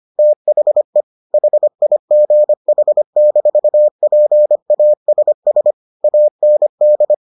Morse code, heard as a keyed tone, about 25 wpm, 600 Hz. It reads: THE HIGH-PASS AND